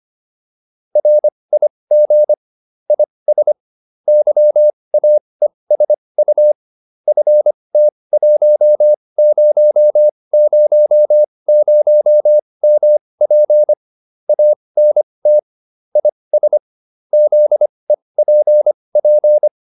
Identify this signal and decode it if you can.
Morse code, heard as a keyed tone, about 25 wpm, spaced as for 20 wpm, 600 Hz. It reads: RIG IS YAESU FT1000MP ANT IS ZEPP